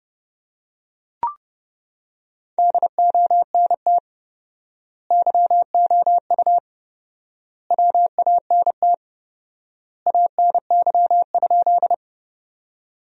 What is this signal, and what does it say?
Morse code, heard as a keyed tone, about 31 wpm, 700 Hz. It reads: DONT YOU WANT ANY?
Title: 4x Word-Spacing - Sentences from Top 300 Words - 30wpm